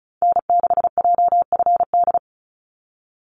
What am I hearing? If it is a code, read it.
Morse code, heard as a keyed tone, about 34 wpm, 700 Hz. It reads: N6JFD
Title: NA SOTA Activator Callsigns 35wpm